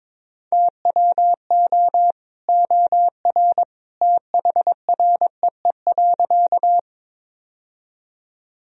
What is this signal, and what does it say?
Morse code, heard as a keyed tone, about 22 wpm, 700 Hz. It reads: TWO OR THREE.
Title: Sentences from Top 1000 Words - Review Part 1 - 22wpm